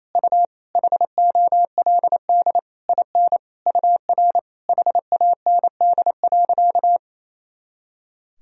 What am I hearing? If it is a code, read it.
Morse code, heard as a keyed tone, about 28 wpm, 700 Hz. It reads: U HOLD IN UR HAND.
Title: Sentences from Top 400 Words and Abbreviations 28wpm